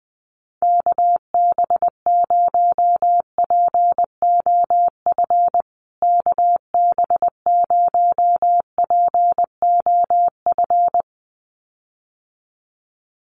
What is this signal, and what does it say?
Morse code, heard as a keyed tone, about 20 wpm, 700 Hz. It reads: KB0POF KB0POF
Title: USA Call Signs Repeated 2x 20wpm